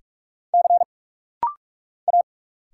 Morse code, heard as a keyed tone, about 46 wpm, 700 Hz. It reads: C A
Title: Single Letter-Number Alt - Rapid-Fire 45wpm